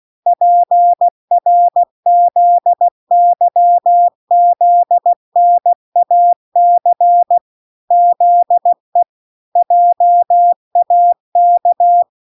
Morse code, heard as a keyed tone, about 16 wpm, 700 Hz. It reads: PRZYZNAC ZE JAK